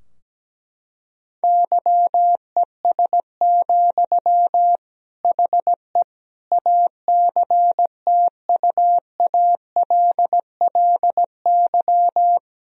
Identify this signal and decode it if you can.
Morse code, heard as a keyed tone, about 17 wpm, 700 Hz. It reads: YES, HE ACTUALLY
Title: Sentences from Top 1000 Words - Review Part 4 - 17wpm